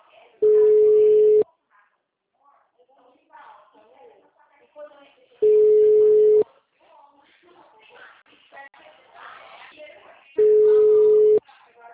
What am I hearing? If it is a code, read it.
Ringback tone (Europe, 425 Hz), with speech.